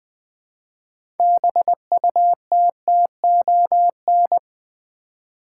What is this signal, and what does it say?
Morse code, heard as a keyed tone, about 20 wpm, 700 Hz. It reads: BUTTON
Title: 6 Letter Words - First 3 Letters Given - 20wpm